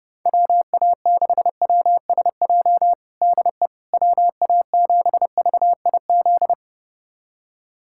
Morse code, heard as a keyed tone, about 30 wpm, 700 Hz. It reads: WA6WSJ DE WA7VIZ
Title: USA Callsign DE Callsign - 6 Characters - 30wpm